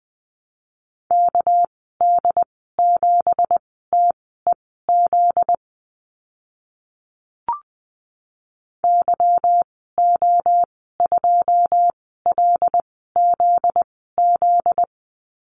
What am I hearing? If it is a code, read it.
Morse code, heard as a keyed tone, about 20 wpm, spaced as for 14 wpm, 700 Hz. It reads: KD7TEZ YO2LZZ
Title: 245 Lesson 39g - Letter z - Call Signs 20wpm